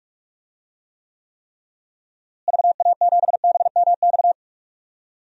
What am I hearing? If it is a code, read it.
Morse code, heard as a keyed tone, about 41 wpm, 700 Hz. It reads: VA7BMX
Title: DXCC-100 Call Signs - 6 Characters - Repeated 3x - 45wpm